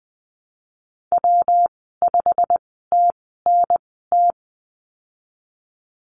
Morse code, heard as a keyed tone, about 20 wpm, spaced as for 15 wpm, 700 Hz. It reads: W5TNT
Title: 147 Lesson 24g - Letter y - Call Signs 20wpm